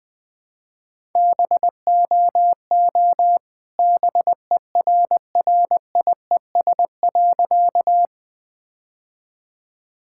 Morse code, heard as a keyed tone, about 20 wpm, 700 Hz. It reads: BOO BERRIES.
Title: All-Age Appropriate Jokes - 20wpm